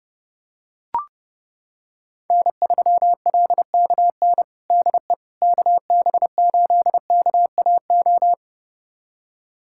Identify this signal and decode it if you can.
Morse code, heard as a keyed tone, about 30 wpm, 700 Hz. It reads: N3LKN DE KB8KAO
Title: USA Callsign DE Callsign 30wpm